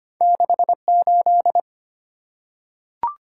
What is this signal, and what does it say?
Morse code, heard as a keyed tone, about 25 wpm, 700 Hz. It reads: 6 8
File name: Two Letter-Number Combinations - 25wpm